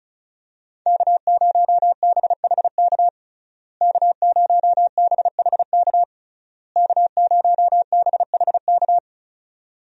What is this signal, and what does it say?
Morse code, heard as a keyed tone, about 36 wpm, 700 Hz. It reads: K0BHK K0BHK K0BHK
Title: USA Call Signs - 5 Characters - Repeated 3x - 35wpm